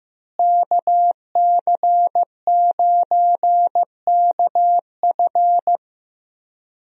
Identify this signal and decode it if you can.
Morse code, heard as a keyed tone, about 15 wpm, 700 Hz. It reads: KC9KF
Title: USA Call Signs - 5 Characters - Repeated 2x - 15wpm